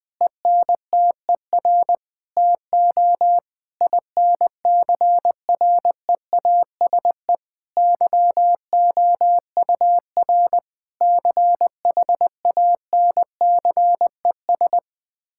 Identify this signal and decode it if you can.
Morse code, heard as a keyed tone, about 20 wpm, 700 Hz. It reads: ENTER TO INCREASE YOUR CHANCES
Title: Sets of 8 Words - 20wpm